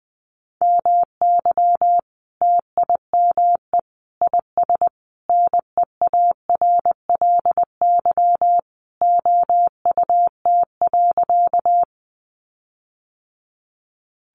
Morse code, heard as a keyed tone, about 20 wpm, 700 Hz. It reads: MY TIME IS NEARLY OUT.
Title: Sentences from Top 1300 Words 20wpm